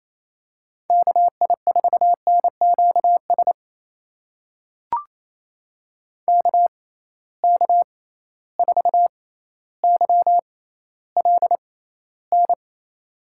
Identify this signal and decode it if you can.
Morse code, heard as a keyed tone, about 28 wpm, spaced as for 14 wpm, 700 Hz. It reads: KI4NQS KK4YLN
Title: USA Call Signs - 6x Character Spacing - 28wpm